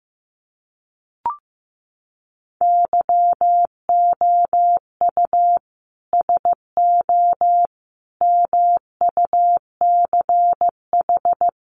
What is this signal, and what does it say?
Morse code, heard as a keyed tone, about 15 wpm, 700 Hz. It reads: YOU SO MUCH
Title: Sets of 5 Words Encore2 15wpm